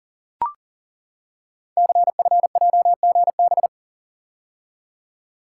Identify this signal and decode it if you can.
Morse code, heard as a keyed tone, about 42 wpm, 700 Hz. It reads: CFJGB
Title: Code Groups - 5 Letters - 40wpm